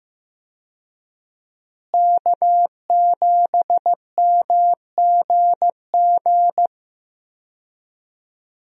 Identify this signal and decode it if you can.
Morse code, heard as a keyed tone, about 16 wpm, 700 Hz. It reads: K7MGG
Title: USA Call Signs - 5 Characters - Repeated 3x - 15wpm